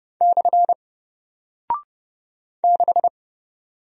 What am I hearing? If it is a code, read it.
Morse code, heard as a keyed tone, about 30 wpm, 700 Hz. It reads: / 6
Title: Single Character (Letter:Number:Punctuation:Prosign) Alt 30wpm